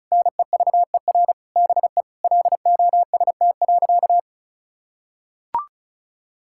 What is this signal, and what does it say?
Morse code, heard as a keyed tone, about 34 wpm, 700 Hz. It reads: NEVER BE LOST.
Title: Sentences from Top 1600 Words 35wpm